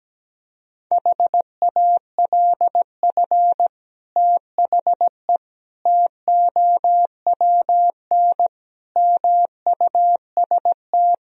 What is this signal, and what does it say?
Morse code, heard as a keyed tone, about 17 wpm, 700 Hz. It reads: HALF THE TOWN MUST
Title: Sentences from Top 1000 Words - Review Part 2 - 17wpm